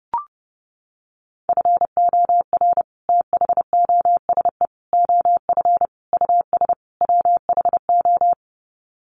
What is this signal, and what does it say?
Morse code, heard as a keyed tone, about 30 wpm, 700 Hz. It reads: FOR THOSE OF US WHO
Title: Sets of 5 Words Encore 30wpm